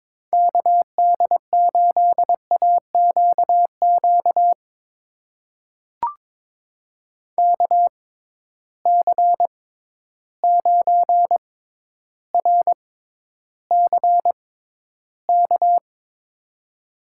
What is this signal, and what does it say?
Morse code, heard as a keyed tone, about 22 wpm, spaced as for 11 wpm, 700 Hz. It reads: KD8AQQ KC9RCK
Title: USA Call Signs - 6 Characters - 6x Character Spacing - 22wpm